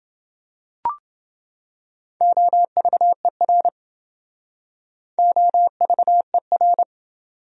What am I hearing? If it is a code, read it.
Morse code, heard as a keyed tone, about 28 wpm, 700 Hz. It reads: OVER OVER